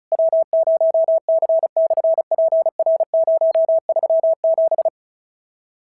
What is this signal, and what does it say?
Morse code, heard as a keyed tone, about 36 wpm, 650 Hz. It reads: W0C/PR037
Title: SOTA QSO Practice 35wpm